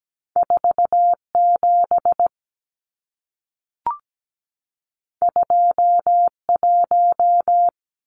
Morse code, heard as a keyed tone, about 17 wpm, 700 Hz. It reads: 47 21
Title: Numbers - 2-Digit 17wpm